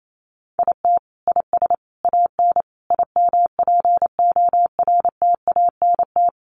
Morse code, heard as a keyed tone, about 28 wpm, 700 Hz. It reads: IT IS AN IMPORTANT
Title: Sentences from Top 1300 Words 28wpm